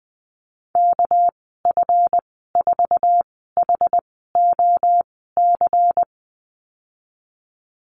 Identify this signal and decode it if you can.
Morse code, heard as a keyed tone, about 20 wpm, spaced as for 15 wpm, 700 Hz. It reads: KF4HOC
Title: 207 Lesson 33g - Letter j - Call Signs 20wpm